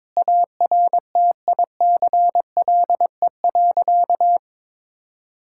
Morse code, heard as a keyed tone, about 22 wpm, 700 Hz. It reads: ARTICLE.
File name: practice / 22wpm / Sentences with 4-Word Phrases - 22wpm